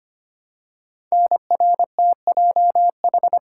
Morse code, heard as a keyed tone, about 25 wpm, 700 Hz. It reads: NRTJH
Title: Code Groups - 5 Letters - 25wpm